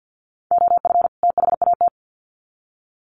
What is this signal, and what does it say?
Morse code, heard as a keyed tone, about 50 wpm, 700 Hz. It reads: OF THAT
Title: Sets of 2 Words 50wpm